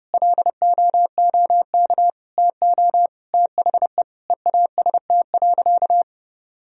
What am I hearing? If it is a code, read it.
Morse code, heard as a keyed tone, about 32 wpm, 700 Hz. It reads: LOOK TO THE EAST.
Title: Sentences from Top 1000 Words - Review Part 3 - 30wpm